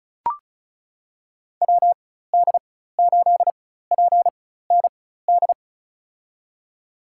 Morse code, heard as a keyed tone, about 34 wpm, spaced as for 16 wpm, 700 Hz. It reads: WD8PND
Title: USA Call Signs - 4x Character Spacing - 35wpm